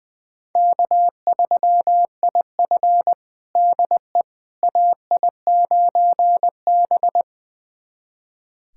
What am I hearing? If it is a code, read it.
Morse code, heard as a keyed tone, about 20 wpm, 700 Hz. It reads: K3IF DE AI9B
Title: USA Callsign DE Callsign - 4 Characters - 20wpm